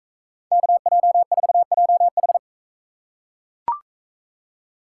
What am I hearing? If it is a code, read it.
Morse code, heard as a keyed tone, about 44 wpm, 700 Hz. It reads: KJ4JH